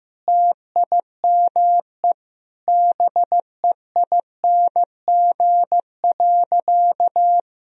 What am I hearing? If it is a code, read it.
Morse code, heard as a keyed tone, about 15 wpm, 700 Hz. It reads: TIME BEING.
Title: Sentences from Top 1000 Words - Review Part 3 - 15wpm